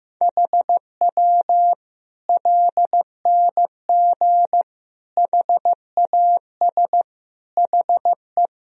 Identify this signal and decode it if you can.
Morse code, heard as a keyed tone, about 15 wpm, 700 Hz. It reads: HW LNG HAS HE
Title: Sentences from Top 100 Words and Abbreviations 15wpm